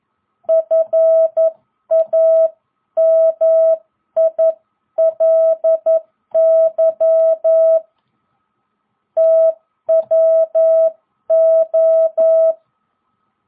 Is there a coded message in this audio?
Yes: Morse code from 0.5 to 12.5 s, heard as a keyed tone, about 11 wpm, 660 Hz. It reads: FAMILY TWO